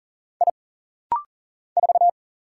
Morse code, heard as a keyed tone, about 40 wpm, 700 Hz. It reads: I 4
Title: Single Letter-Number - Rapid-Fire 40wpm